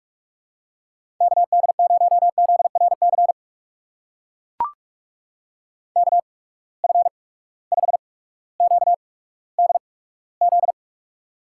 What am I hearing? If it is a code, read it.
Morse code, heard as a keyed tone, about 42 wpm, spaced as for 19 wpm, 700 Hz. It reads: KD0ZRC KF5QDZ
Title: USA Call Signs - 6 Characters - 8x Character Spacing - 45wpm